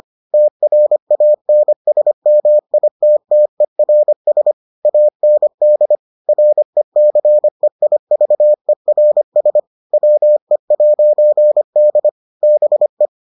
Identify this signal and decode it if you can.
Morse code, heard as a keyed tone, about 25 wpm, 600 Hz. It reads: TRANSMITTERS AND RECEIVERS WE'D BE